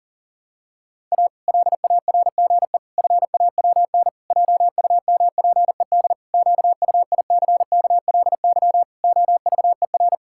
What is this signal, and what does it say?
Morse code, heard as a keyed tone, about 40 wpm, 700 Hz. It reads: A LARGE FAWN JUMPED QUICKLY OVER
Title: Pangram Challenge v2 - 40wpm